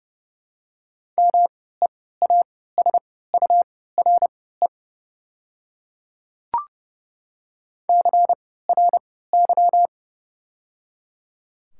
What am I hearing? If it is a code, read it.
Morse code, heard as a keyed tone, about 30 wpm, spaced as for 17 wpm, 700 Hz. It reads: MEASURE CRY
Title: 3x Character Spacing - Top 500 Words - 30wpm